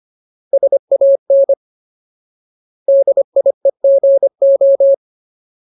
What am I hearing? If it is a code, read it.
Morse code, heard as a keyed tone, about 25 wpm, 550 Hz. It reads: SAN DIEGO
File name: Ultimate Word List / 25WPM Ultimate